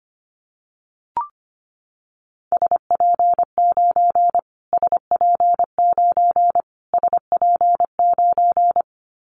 Morse code, heard as a keyed tone, about 25 wpm, 700 Hz. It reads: SP9 SP9 SP9
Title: Callsign Prefixes for Top 25 DXCC - Repeated 3x - 25wpm